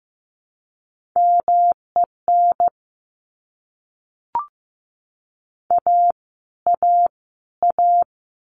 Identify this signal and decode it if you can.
Morse code, heard as a keyed tone, about 15 wpm, 700 Hz. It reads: MEN A A A
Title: Top 500 Words Repeated 15wpm